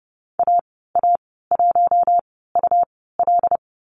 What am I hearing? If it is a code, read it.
Morse code, heard as a keyed tone, about 30 wpm, spaced as for 17 wpm, 700 Hz. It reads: AA1UL
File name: Lesson 19g - Letter w - Call Signs 30wpm